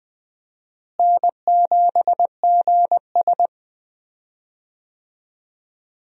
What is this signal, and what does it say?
Morse code, heard as a keyed tone, about 21 wpm, 700 Hz. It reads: N7GS
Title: USA Call Signs - 4 Characters - Repeated 2x - 20wpm